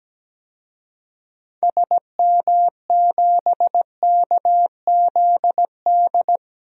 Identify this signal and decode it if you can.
Morse code, heard as a keyed tone, about 17 wpm, 700 Hz. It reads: SM7KZD
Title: DXCC-100 Call Signs - 6 Characters - 17wpm